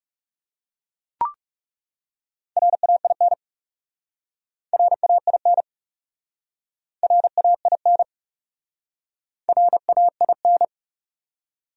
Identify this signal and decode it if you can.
Morse code, heard as a keyed tone, about 40 wpm, 700 Hz. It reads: RAIN RAIN RAIN RAIN